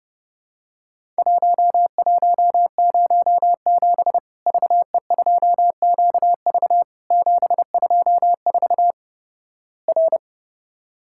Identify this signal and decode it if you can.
Morse code, heard as a keyed tone, about 30 wpm, 700 Hz. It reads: 1107 VE2QV 724 R
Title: Worked All Europe DX Contest - QTC Exchange - 30wpm